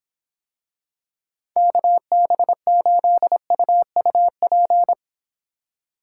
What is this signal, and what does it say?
Morse code, heard as a keyed tone, about 26 wpm, 700 Hz. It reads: KB8UUP